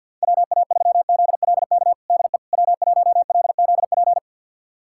Morse code, heard as a keyed tone, about 50 wpm, 700 Hz. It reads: WA3ZLK DE W1LZP
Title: USA Callsign DE Callsign 50wpm